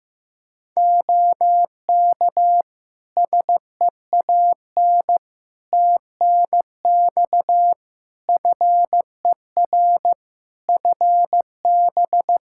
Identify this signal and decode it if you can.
Morse code, heard as a keyed tone, about 15 wpm, 700 Hz. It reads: OK SEAN TNX FER FB
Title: Practice QSO - Exchange 4 - Ending - 15wpm